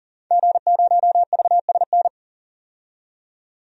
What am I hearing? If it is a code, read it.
Morse code, heard as a keyed tone, about 40 wpm, 700 Hz. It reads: G0VSN
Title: DXCC-100 Call Signs - 5 Characters - 40wpm